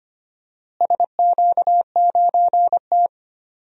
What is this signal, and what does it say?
Morse code, heard as a keyed tone, about 25 wpm, 700 Hz. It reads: SQ9T
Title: DXCC-100 Call Signs - 4 Characters - Repeated 3x - 25wpm